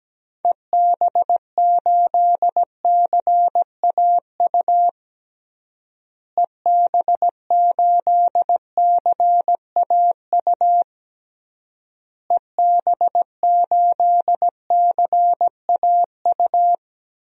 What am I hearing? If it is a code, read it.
Morse code, heard as a keyed tone, about 17 wpm, 700 Hz. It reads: EB8CAU EB8CAU EB8CAU
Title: DXCC-100 Call Signs - 6 Characters - Repeated 3x - 17wpm